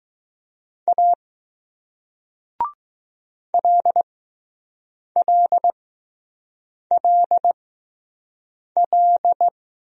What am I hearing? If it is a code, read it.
Morse code, heard as a keyed tone, about 20 wpm, 700 Hz. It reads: A L L L L